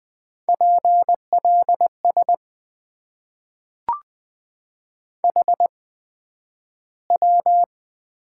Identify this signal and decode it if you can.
Morse code, heard as a keyed tone, about 20 wpm, spaced as for 11 wpm, 700 Hz. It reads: PLS HW
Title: Top 90 QSO Elements - 8x Character Spacing - 20wpm